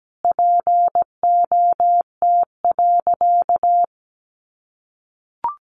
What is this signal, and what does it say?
Morse code, heard as a keyed tone, about 17 wpm, 700 Hz. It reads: POT.
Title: Sentences from Top 1000 Words 17wpm